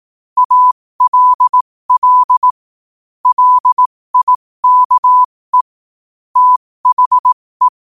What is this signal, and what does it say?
Morse code, heard as a keyed tone, about 18 wpm, spaced as for 15 wpm, 1000 Hz. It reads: ALL LIKE THE
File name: qotdfiles / 15wpm